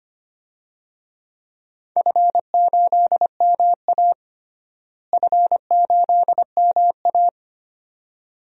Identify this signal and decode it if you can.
Morse code, heard as a keyed tone, about 25 wpm, 700 Hz. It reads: F8MA F8MA
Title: DXCC-100 Call Signs - 4 Characters - Repeated 2x - 25wpm